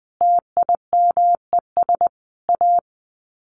Morse code, heard as a keyed tone, about 19 wpm, 700 Hz. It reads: TIMES A
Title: Sets of 5 Words Encore2 20wpm